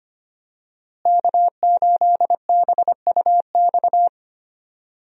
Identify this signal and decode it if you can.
Morse code, heard as a keyed tone, about 25 wpm, 700 Hz. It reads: K8BUX
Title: USA Call Signs - 5 Characters - Repeated 3x - 25wpm